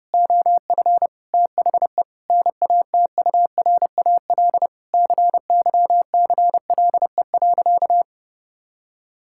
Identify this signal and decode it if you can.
Morse code, heard as a keyed tone, about 30 wpm, 700 Hz. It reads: OF THE NATURAL CYCLE.
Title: Sentences from Top 1000 Words - Review Part 6 - 30wpm